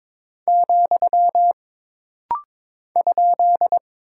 Morse code, heard as a keyed tone, about 22 wpm, 700 Hz. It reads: , ?
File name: practice / 22wpm / Single Punctuation-Prosign - Rapid-Fire - 22wpm